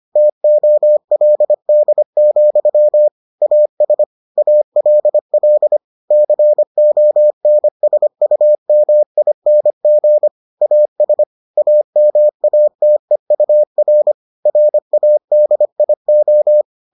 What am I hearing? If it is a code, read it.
Morse code, heard as a keyed tone, about 25 wpm, 600 Hz. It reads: TOLD, AS ALL CONSUMING AS AMATEUR RADIO